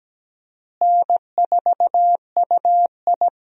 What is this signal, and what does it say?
Morse code, heard as a keyed tone, about 17 wpm, 700 Hz. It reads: N4UI
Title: USA Call Signs - 4 Characters - Repeated 2x - 17wpm